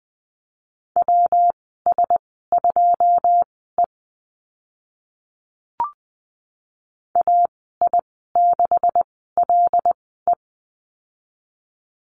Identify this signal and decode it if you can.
Morse code, heard as a keyed tone, about 20 wpm, spaced as for 14 wpm, 700 Hz. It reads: WS2E AI6LE
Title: 140 Lesson 23g - Letter f - Call Signs 20wpm